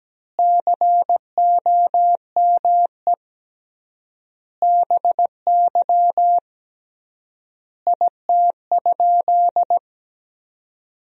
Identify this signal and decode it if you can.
Morse code, heard as a keyed tone, about 17 wpm, 700 Hz. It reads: COME BY IT?
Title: 3x Word-Spacing - Sentences from Top 100 Words - 17wpm